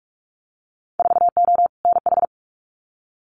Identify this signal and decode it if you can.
Morse code, heard as a keyed tone, about 46 wpm, 700 Hz. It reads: 4O NH
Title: Winter Field Day Exchanges Encore 45wpm